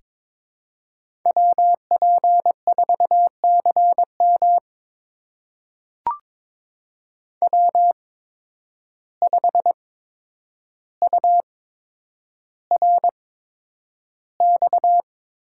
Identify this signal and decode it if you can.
Morse code, heard as a keyed tone, about 22 wpm, spaced as for 9 wpm, 700 Hz. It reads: WP4CM W5URX
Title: USA Call Signs - 8x Character Spacing - 22wpm